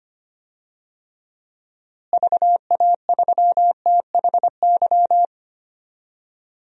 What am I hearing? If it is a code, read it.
Morse code, heard as a keyed tone, about 25 wpm, 700 Hz. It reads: VA3THY
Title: DXCC-100 Call Signs - 6 Characters - Repeated 3x - 25wpm